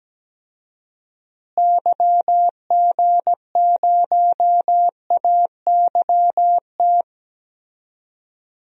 Morse code, heard as a keyed tone, about 17 wpm, 700 Hz. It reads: YG0AYT
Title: DXCC-100 Call Signs - 6 Characters - Repeated 3x - 17wpm